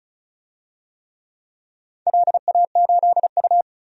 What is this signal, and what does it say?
Morse code, heard as a keyed tone, about 36 wpm, 700 Hz. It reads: LA8U
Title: DXCC-100 Call Signs - 4 Characters - 35wpm